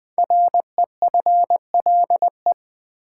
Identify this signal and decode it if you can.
Morse code, heard as a keyed tone, about 20 wpm, 700 Hz. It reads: REFLE
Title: Build A Word - 5 Letters At A Time - 10-Letter Words - 20wpm